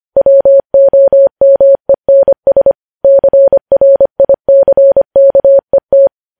Morse code, heard as a keyed tone, about 25 wpm, 550 Hz. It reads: WOMENS CRICKET
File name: podcast / News headlines-1x-25x25